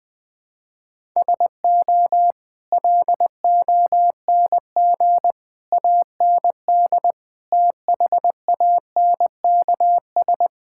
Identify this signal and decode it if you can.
Morse code, heard as a keyed tone, about 20 wpm, 700 Hz. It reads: SO LONG AND THANKS
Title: Sets of 8 Words - 20wpm